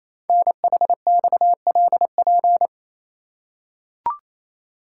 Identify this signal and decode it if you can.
Morse code, heard as a keyed tone, about 29 wpm, 700 Hz. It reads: NHXLP
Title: Code Groups - 5 Letter:Numbers - 28wpm